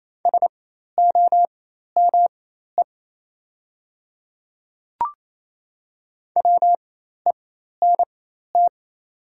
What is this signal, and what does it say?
Morse code, heard as a keyed tone, about 28 wpm, spaced as for 13 wpm, 700 Hz. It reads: SOME WENT
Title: 4x Character Spacing - Top 200 Words - 28wpm